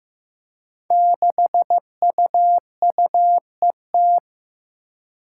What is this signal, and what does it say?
Morse code, heard as a keyed tone, about 15 wpm, 700 Hz. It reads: 6UUET
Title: Five Challenging Letter-Number Combinations 15wpm